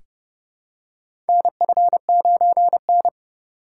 Morse code, heard as a keyed tone, about 30 wpm, 700 Hz. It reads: NF9N
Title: USA Call Signs - 4 Characters - 3x Character Spacing - 30wpm